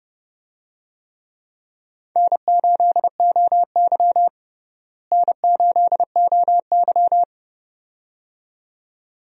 Morse code, heard as a keyed tone, about 30 wpm, 700 Hz. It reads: N8OY N8OY
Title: USA Call Signs - 4 Characters - Repeated 2x - 30wpm